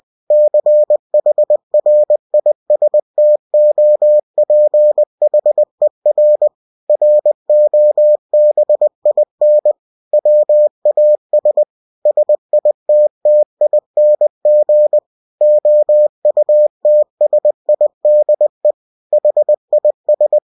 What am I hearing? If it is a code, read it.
Morse code, heard as a keyed tone, about 20 wpm, 600 Hz. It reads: CHRISTOPHER ROBIN WAS SITTING OUTSIDE HIS
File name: cw-books / winnie-the-pooh / winnie-20wpm-page-0076